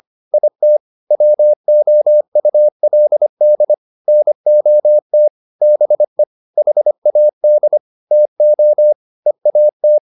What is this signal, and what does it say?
Morse code, heard as a keyed tone, about 25 wpm, 600 Hz. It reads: IT WOULD NOT BE HAD TO EAT
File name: the old man and the sea part 1 25wpm